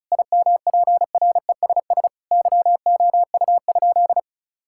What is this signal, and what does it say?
Morse code, heard as a keyed tone, about 36 wpm, 700 Hz. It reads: IMPRESS YOU?
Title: Sentences from Top 1600 Words 35wpm